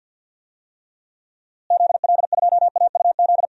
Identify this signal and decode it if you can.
Morse code, heard as a keyed tone, about 47 wpm, 700 Hz. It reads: ZL2AUZ
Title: DXCC-100 Call Signs - 6 Characters - Repeated 3x - 50wpm